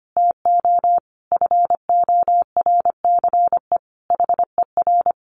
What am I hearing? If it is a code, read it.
Morse code, heard as a keyed tone, about 25 wpm, 700 Hz. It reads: TO FORCE HER
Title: Sentences from Top 1500 Words 25wpm